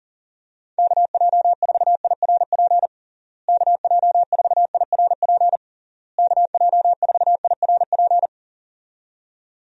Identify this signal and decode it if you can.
Morse code, heard as a keyed tone, about 40 wpm, 700 Hz. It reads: KJ4IRP KJ4IRP KJ4IRP
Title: USA Call Signs - 6 Characters - Repeated 3x - 40wpm